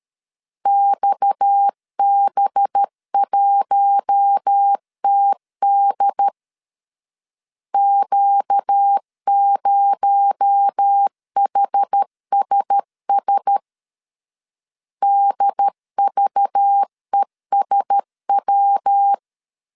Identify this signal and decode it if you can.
Morse code, heard as a keyed tone, about 13 wpm, 800 Hz. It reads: XB1TD Q0HSS DVESW
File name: CW / DET / 1 6 1500C S B 12wpm 3431